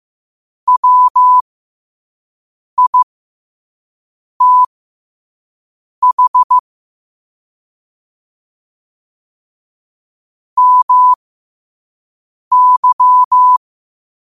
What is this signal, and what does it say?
Morse code, heard as a keyed tone, about 15 wpm, spaced as for 5 wpm, 1000 Hz. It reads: WITH MY
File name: qotdfiles / 5wpm